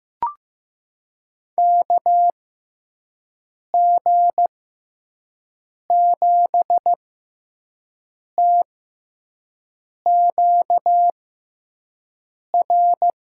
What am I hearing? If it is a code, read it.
Morse code, heard as a keyed tone, about 15 wpm, spaced as for 5 wpm, 700 Hz. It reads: KG7TQR
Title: USA Call Signs - 6 Characters - 6x Character Spacing - 15wpm